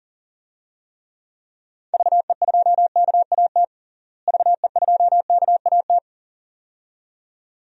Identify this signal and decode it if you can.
Morse code, heard as a keyed tone, about 37 wpm, 700 Hz. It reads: VE2KAT VE2KAT
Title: DXCC-100 Call Signs - 6 Characters - Repeated 2x - 40wpm